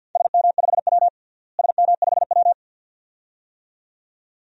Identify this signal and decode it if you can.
Morse code, heard as a keyed tone, about 47 wpm, 700 Hz. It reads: SM5W SM5W
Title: DXCC-100 Call Signs - 4 Characters - Repeated 2x - 50wpm